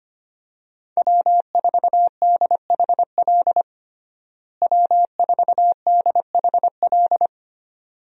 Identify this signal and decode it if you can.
Morse code, heard as a keyed tone, about 25 wpm, 690 Hz. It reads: W4DHL W4DHL
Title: USA Call Signs - 5 Characters - Repeated 2x - 25wpm